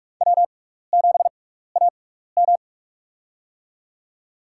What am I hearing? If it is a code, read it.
Morse code, heard as a keyed tone, about 46 wpm, spaced as for 16 wpm, 700 Hz. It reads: W7AM
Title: USA Call Signs - 4 Characters - 6x Character Spacing - 45wpm